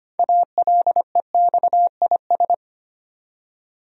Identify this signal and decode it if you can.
Morse code, heard as a keyed tone, about 25 wpm, 700 Hz. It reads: ALEXIS
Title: USA Common Names - 25wpm